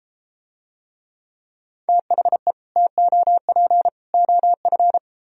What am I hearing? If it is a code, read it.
Morse code, heard as a keyed tone, about 32 wpm, 700 Hz. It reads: THE TOP OF